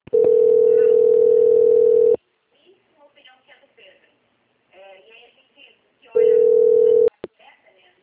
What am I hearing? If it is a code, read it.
Ringback tone (North America), 440 + 480 Hz, with speech.
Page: Recife - PE